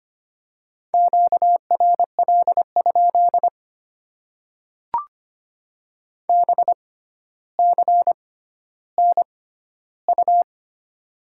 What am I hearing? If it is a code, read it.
Morse code, heard as a keyed tone, about 25 wpm, spaced as for 13 wpm, 700 Hz. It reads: QRL? BCNU
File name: Top 90 QSO Elements - 6x Character Spacing - 25wpm